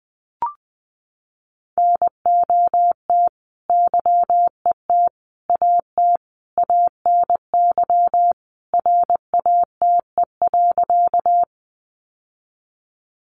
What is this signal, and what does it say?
Morse code, heard as a keyed tone, about 20 wpm, 700 Hz. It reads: NOT YET AT ANY RATE.
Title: Sentences from Top 700 Words 20wpm